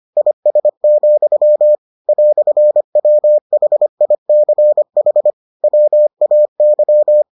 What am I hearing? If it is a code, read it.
Morse code, heard as a keyed tone, about 25 wpm, 600 Hz. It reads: IS, "WHICH WAY